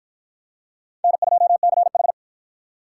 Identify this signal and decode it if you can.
Morse code, heard as a keyed tone, about 50 wpm, 700 Hz. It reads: N2CH